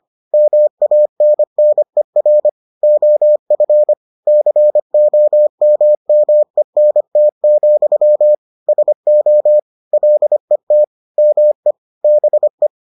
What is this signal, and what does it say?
Morse code, heard as a keyed tone, about 25 wpm, 600 Hz. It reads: MANNER OF COMMENT, SO LET ME BE